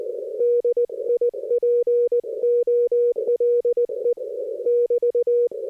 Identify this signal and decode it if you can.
Morse code, heard as a keyed tone, about 20 wpm, 480 Hz. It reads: DIPOLE <BT>
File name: 60m qso